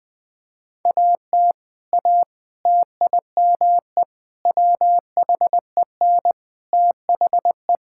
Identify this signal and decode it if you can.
Morse code, heard as a keyed tone, about 20 wpm, 700 Hz. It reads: AT A TIME WHEN THE
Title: Sentences with 5-Word Phrases - 20wpm